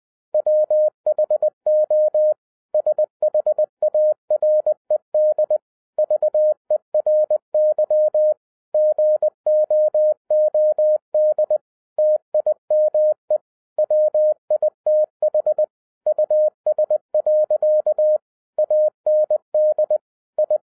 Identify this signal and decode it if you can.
Morse code, heard as a keyed tone, about 20 wpm, 610 Hz. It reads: WHO SHARED VERY GOOD TIME WITH US. AND I
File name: 2011-07-mp3 / n48 JOHN K3TN 20wpm